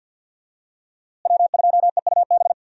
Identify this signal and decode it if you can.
Morse code, heard as a keyed tone, about 47 wpm, 700 Hz. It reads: W2EUB
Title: USA Call Signs - 5 Characters - Repeated 2x - 50wpm